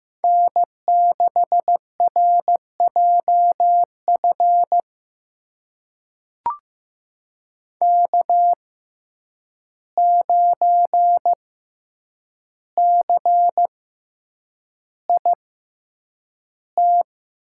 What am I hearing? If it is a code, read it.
Morse code, heard as a keyed tone, about 15 wpm, spaced as for 8 wpm, 700 Hz. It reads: N6RJF K9CIT
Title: USA Call Signs - 5 Characters - 6x Character Spacing - 15wpm